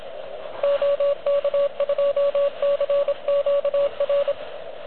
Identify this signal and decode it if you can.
Morse code, heard as a keyed tone, about 25 wpm, 590 Hz, with radio noise. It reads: OK2CQR
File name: OK QRP 2022 / Zvuk / REC37